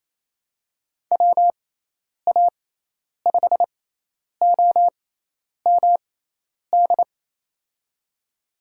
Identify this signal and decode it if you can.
Morse code, heard as a keyed tone, about 28 wpm, spaced as for 10 wpm, 700 Hz. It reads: WA5OMD
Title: USA Call Signs - 6x Character Spacing - 28wpm